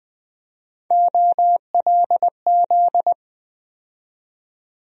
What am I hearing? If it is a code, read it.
Morse code, heard as a keyed tone, about 20 wpm, 700 Hz. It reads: OLZ
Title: USA Call Sign Postfixes - 3 Letters 20wpm